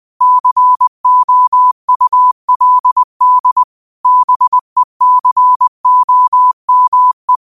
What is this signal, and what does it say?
Morse code, heard as a keyed tone, about 20 wpm, 1000 Hz. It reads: COULD BECOME
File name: qotdfiles / 20wpm